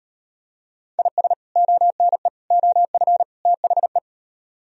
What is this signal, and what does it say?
Morse code, heard as a keyed tone, about 38 wpm, 700 Hz. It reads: IS ONE OF THE